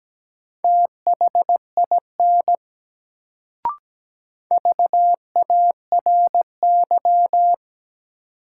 Morse code, heard as a keyed tone, about 17 wpm, 700 Hz. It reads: THIN VARY
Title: Sentences from Top 1200 Words - New Word Drill Alt - 17wpm